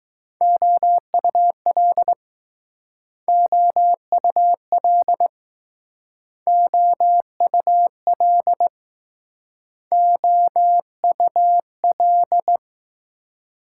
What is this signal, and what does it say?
Morse code, heard as a keyed tone, about 20 wpm, 700 Hz. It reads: OUL OUL OUL OUL